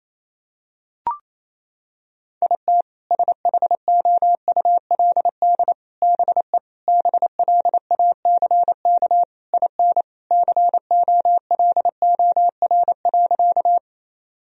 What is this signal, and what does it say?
Morse code, heard as a keyed tone, about 28 wpm, 700 Hz. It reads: IT SHOULD BE BLACK IN COLOR.
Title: Sentences from Top 1000 Words - Review Part 2 - 28wpm